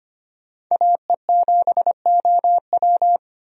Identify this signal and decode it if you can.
Morse code, heard as a keyed tone, about 25 wpm, 690 Hz. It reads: AE7OW
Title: USA Call Signs - 5 Characters - 25wpm